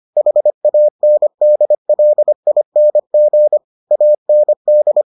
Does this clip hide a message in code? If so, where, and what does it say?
Yes: Morse code from 0.2 to 5.0 s, heard as a keyed tone, about 25 wpm, 600 Hz. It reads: HANDLING AND